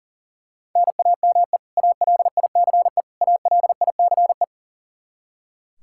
Morse code, heard as a keyed tone, about 40 wpm, 700 Hz. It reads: NAME ALICE ALICE
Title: Practice QSO - Exchange 1 - Beginning - 40wpm